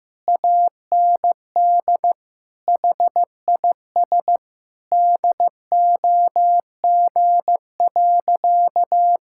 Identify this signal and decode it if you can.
Morse code, heard as a keyed tone, about 15 wpm, 700 Hz. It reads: AND HIS DOG.
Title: Sentences from Top 1600 Words 15wpm